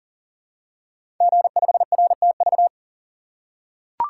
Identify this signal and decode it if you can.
Morse code, heard as a keyed tone, about 42 wpm, 700 Hz. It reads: G5RTV
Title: Code Groups - 5 Letter:Numbers - 40wpm